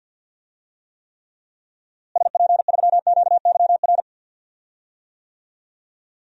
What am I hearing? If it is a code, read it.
Morse code, heard as a keyed tone, about 52 wpm, 700 Hz. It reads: SP3QYR
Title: DXCC-100 Call Signs - 6 Characters - Repeated 3x - 50wpm